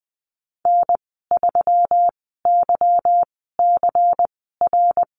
Morse code, heard as a keyed tone, about 20 wpm, 700 Hz. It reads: N 3 Y C R
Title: 154 Lesson 25g - Letter p - Call Signs 20wpm